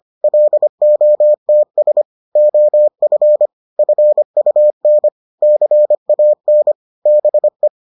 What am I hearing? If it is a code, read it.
Morse code, heard as a keyed tone, about 25 wpm, 600 Hz. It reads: LOTS OF FUN CAN BE